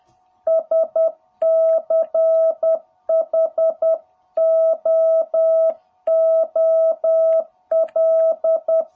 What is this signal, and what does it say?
Morse code, heard as a keyed tone, about 10 wpm, 660 Hz. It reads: SCHOOL
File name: jh8cbh / 2021cw